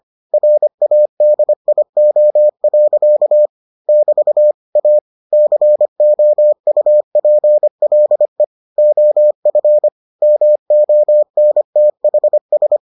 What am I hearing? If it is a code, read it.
Morse code, heard as a keyed tone, about 25 wpm, 600 Hz. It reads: RADIO. <BT> A COUPLE OF MONTHS